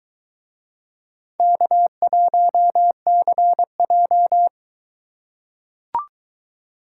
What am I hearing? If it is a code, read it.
Morse code, heard as a keyed tone, about 21 wpm, 700 Hz. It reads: K1CJ